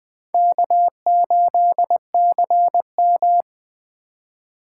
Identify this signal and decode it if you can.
Morse code, heard as a keyed tone, about 20 wpm, 700 Hz. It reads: K8CM
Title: Speed Racing - USA 1x2 Call Signs 17wpm